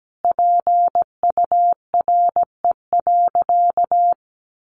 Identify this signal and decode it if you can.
Morse code, heard as a keyed tone, about 17 wpm, 700 Hz. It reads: PURE.
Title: Sentences from Top 1500 Words 17wpm